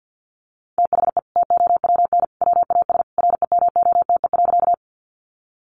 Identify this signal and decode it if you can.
Morse code, heard as a keyed tone, about 50 wpm, 700 Hz. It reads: THE TOWN WAS REMOTE.
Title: Sentences from Top 1300 Words 50wpm